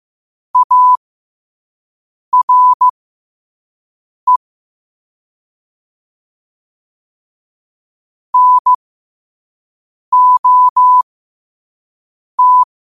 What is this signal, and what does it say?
Morse code, heard as a keyed tone, about 15 wpm, spaced as for 5 wpm, 1000 Hz. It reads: ARE NOT
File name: qotdfiles / 5wpm